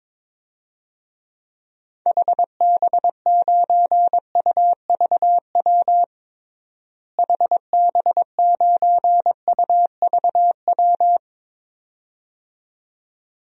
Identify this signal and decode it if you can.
Morse code, heard as a keyed tone, about 22 wpm, 700 Hz. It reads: HB9UVW HB9UVW
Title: DXCC-100 Call Signs - 6 Characters - Repeated 2x - 22wpm